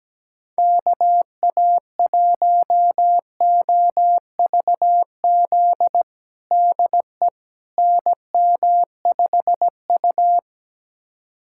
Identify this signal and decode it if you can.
Morse code, heard as a keyed tone, about 17 wpm, 700 Hz. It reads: KA1OVZ DE NM5U
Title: USA Callsign DE Callsign 17wpm